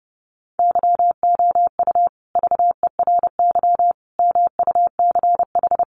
Morse code, heard as a keyed tone, about 30 wpm, 700 Hz. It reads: YOU VERY MUCH
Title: Sets of 5 Words Encore 30wpm